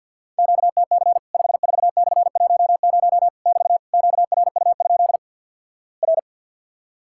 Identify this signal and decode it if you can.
Morse code, heard as a keyed tone, about 50 wpm, 700 Hz. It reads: QTC 54/10 <BT> QRU? R
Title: Worked All Europe DX Contest - QTC Exchange with Repeats - 50wpm